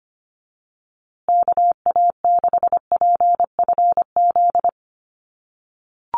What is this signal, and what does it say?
Morse code, heard as a keyed tone, about 25 wpm, 700 Hz. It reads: KA6PFZ
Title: USA Call Signs Repeated 2x 25wpm